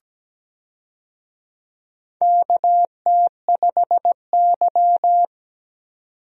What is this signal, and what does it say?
Morse code, heard as a keyed tone, about 17 wpm, 700 Hz. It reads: KT5Y
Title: USA Call Signs - 4 Characters - 17wpm